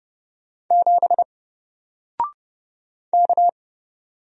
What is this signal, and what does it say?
Morse code, heard as a keyed tone, about 30 wpm, 700 Hz. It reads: 7 K
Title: Single Letter-Number 30wpm